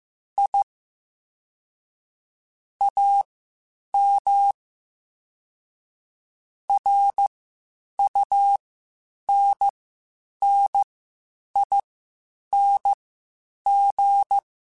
Morse code, heard as a keyed tone, about 15 wpm, spaced as for 8 wpm, 800 Hz. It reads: I AM RUNNING
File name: (spaced) / QSOs Spaced-mp3s / WD0BGP 1 @13 sp